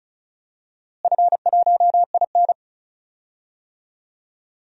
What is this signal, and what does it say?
Morse code, heard as a keyed tone, about 34 wpm, 700 Hz. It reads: F1IN
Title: DXCC-100 Call Signs - 4 Characters - Repeated 2x - 35wpm